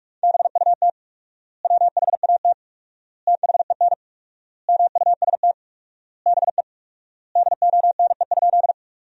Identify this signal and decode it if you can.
Morse code, heard as a keyed tone, about 44 wpm, 700 Hz. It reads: BUT WHAT THEN MUST BE DONE?
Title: 4x Word-Spacing - Sentences from Top 500 Words - 45wpm